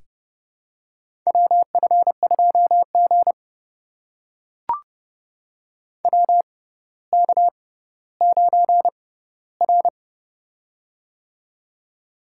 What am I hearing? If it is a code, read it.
Morse code, heard as a keyed tone, about 28 wpm, spaced as for 15 wpm, 700 Hz. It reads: WF2G WK9R
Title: USA Call Signs - 4 Characters - 6x Character Spacing - 30wpm